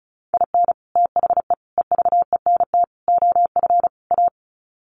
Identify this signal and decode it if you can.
Morse code, heard as a keyed tone, about 34 wpm, 700 Hz. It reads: IN THE EVENT OF A
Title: Sets of 5 Words Encore 35wpm